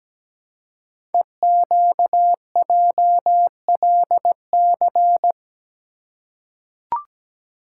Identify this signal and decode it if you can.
Morse code, heard as a keyed tone, about 17 wpm, 700 Hz. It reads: EQJLC
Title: Code Groups - 5 Letter:Numbers - 17wpm